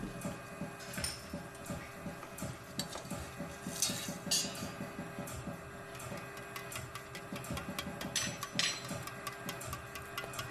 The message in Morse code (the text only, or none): none